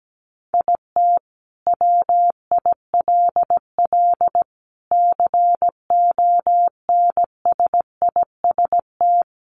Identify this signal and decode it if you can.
Morse code, heard as a keyed tone, about 17 wpm, 700 Hz. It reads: IT WILL CONSIST
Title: Sentences from Top 1400 Words 17wpm